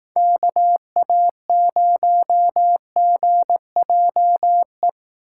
Morse code, heard as a keyed tone, about 18 wpm, 700 Hz. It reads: KA0GJE
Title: Speed Racing - USA Call Signs 15wpm